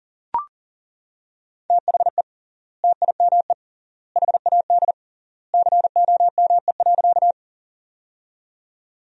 Morse code, heard as a keyed tone, about 40 wpm, 700 Hz. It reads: THE TIME HAD COME.
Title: 3x Word-Spacing - Sentences from Top 100 Words - 40wpm